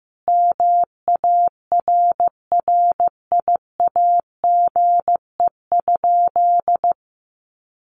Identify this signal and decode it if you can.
Morse code, heard as a keyed tone, about 15 wpm, 700 Hz. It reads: MARRIAGE?
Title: Sentences from Top 1200 Words 15wpm